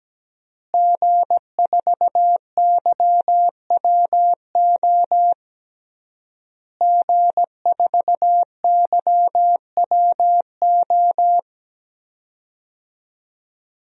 Morse code, heard as a keyed tone, about 17 wpm, 700 Hz. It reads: G4YWO G4YWO
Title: DXCC-100 Call Signs - 5 Characters - Repeated 3x - 17wpm